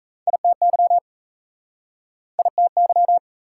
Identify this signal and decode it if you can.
Morse code, heard as a keyed tone, about 41 wpm, 700 Hz. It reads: ITY ITY